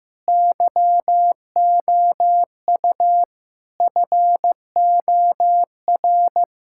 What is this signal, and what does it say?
Morse code, heard as a keyed tone, about 15 wpm, 700 Hz. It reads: YOU FOR
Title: Sentences from Top 1000 Words - Review Part 6 - 15wpm